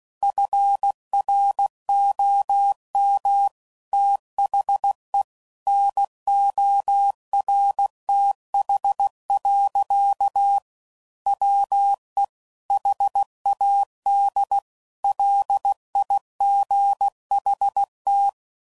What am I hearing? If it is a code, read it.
Morse code, heard as a keyed tone, about 16 wpm, 800 Hz. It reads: FROM THE NORTH. WE HAD LIGHT
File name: CW Code Groups @15WPM/Weather Sets